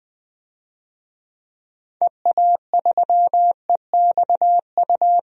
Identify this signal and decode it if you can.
Morse code, heard as a keyed tone, about 19 wpm, 700 Hz. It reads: EA3EXU
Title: DXCC-100 Call Signs - 6 Characters - Repeated 2x - 20wpm